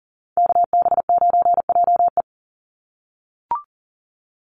Morse code, heard as a keyed tone, about 40 wpm, 700 Hz. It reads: KB9JE
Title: USA Call Signs 40wpm